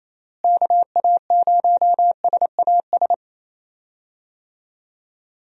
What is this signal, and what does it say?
Morse code, heard as a keyed tone, about 28 wpm, 700 Hz. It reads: KA0SAS
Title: USA Call Signs - 6 Characters - Repeated 3x - 28wpm